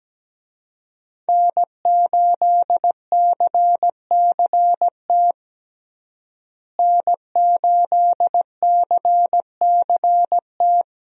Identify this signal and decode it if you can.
Morse code, heard as a keyed tone, about 17 wpm, 700 Hz. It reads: N8CCT N8CCT
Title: USA Call Signs - 5 Characters - Repeated 3x - 17wpm